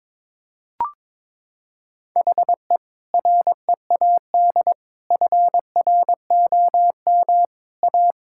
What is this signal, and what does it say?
Morse code, heard as a keyed tone, about 22 wpm, 700 Hz. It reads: HE READ FROM A
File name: practice / 22wpm / Sentences with 2-Word Phrases - 22wpm